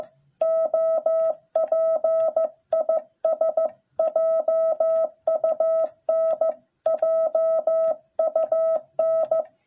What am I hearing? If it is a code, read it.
Morse code, heard as a keyed tone, about 14 wpm, 660 Hz. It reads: OP IS JUN JUN